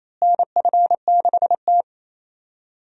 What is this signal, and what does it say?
Morse code, heard as a keyed tone, about 28 wpm, 700 Hz. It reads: NF6T
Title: USA Call Signs - 4 Characters - Repeated 2x - 28wpm